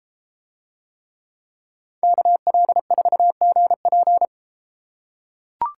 Morse code, heard as a keyed tone, about 32 wpm, 700 Hz. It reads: KL4GP